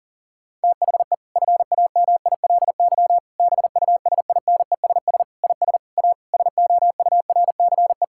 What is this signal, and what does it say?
Morse code, heard as a keyed tone, about 40 wpm, 700 Hz. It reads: THE FAMILY BUSINESS IS A SOURCE
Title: Sentences from Top 1000 Words - Review Part 6 - 40wpm